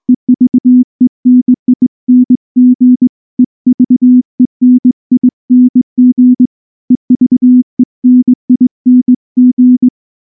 Morse code, heard as a keyed tone, about 20 wpm, 250 Hz. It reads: EVENING EVENING EVENING
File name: morse / morse-training-20WPM-7